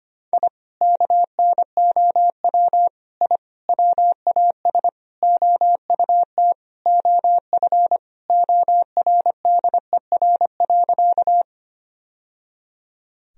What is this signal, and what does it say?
Morse code, heard as a keyed tone, about 25 wpm, 700 Hz. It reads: I KNOW I WAS OUT OF ORDER.
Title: Sentences from Top 1000 Words - Review Part 2 - 25wpm